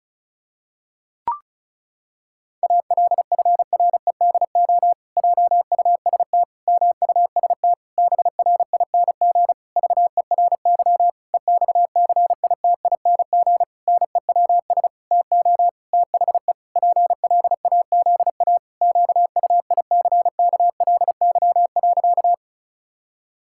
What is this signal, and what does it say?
Morse code, heard as a keyed tone, about 36 wpm, 700 Hz. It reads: ALFREDO JUST MUST BRING VERY EXCITING NEWS TO THE PLAZA QUICKLY.
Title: Pangram Challenge v2 - 35wpm